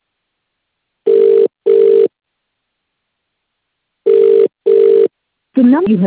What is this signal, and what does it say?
Ringback tone (UK), 400 + 450 Hz, with speech.